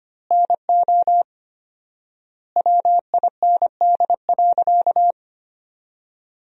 Morse code, heard as a keyed tone, about 25 wpm, 700 Hz. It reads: NO WIND.
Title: 4x Word-Spacing - Sentences from Top 400 Words - 25wpm